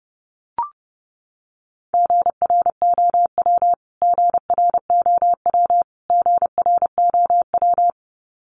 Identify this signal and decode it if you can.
Morse code, heard as a keyed tone, about 30 wpm, 700 Hz. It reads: GROW GROW GROW
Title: Top 500 Words Repeated 30wpm